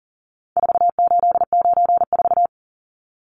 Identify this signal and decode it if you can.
Morse code, heard as a keyed tone, about 40 wpm, 700 Hz. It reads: 4894
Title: Numbers - 4-Digit 40wpm